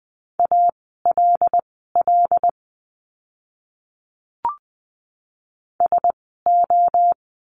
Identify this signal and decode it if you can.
Morse code, heard as a keyed tone, about 20 wpm, spaced as for 14 wpm, 700 Hz. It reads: ALL SO
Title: 146 Lesson 24f - Letter y - Words 20wpm